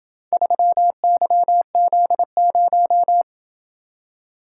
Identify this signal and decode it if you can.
Morse code, heard as a keyed tone, about 27 wpm, 700 Hz. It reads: 3YZ0